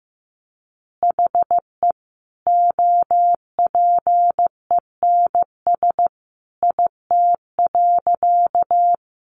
Morse code, heard as a keyed tone, about 15 wpm, 700 Hz. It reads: HE OPENS IT.
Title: Additional Easy Phrases 15wpm